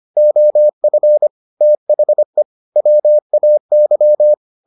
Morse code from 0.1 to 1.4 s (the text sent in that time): OF